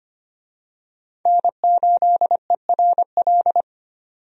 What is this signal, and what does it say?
Morse code, heard as a keyed tone, about 25 wpm, 700 Hz. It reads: N8ERL
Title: USA Call Signs - 5 Characters - 25wpm